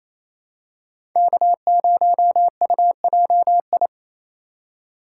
Morse code, heard as a keyed tone, about 28 wpm, 700 Hz. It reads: K0UJI
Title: USA Call Signs - 5 Characters - 4x Character Spacing - 28wpm